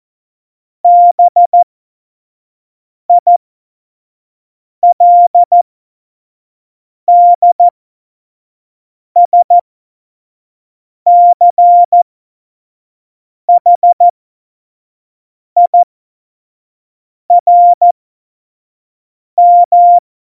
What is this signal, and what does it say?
Morse code, heard as a keyed tone, about 14 wpm, spaced as for 5 wpm, 700 Hz. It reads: BILDSCHIRM
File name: Morsekurs / Witze / Witz 18 Juli 17, Tempo 25 F70